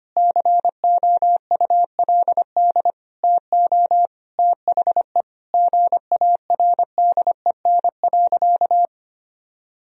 Morse code, heard as a keyed tone, about 25 wpm, 700 Hz. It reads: COULD TO THE GARDEN.
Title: Sentences from Top 1000 Words - Review Part 4 - 25wpm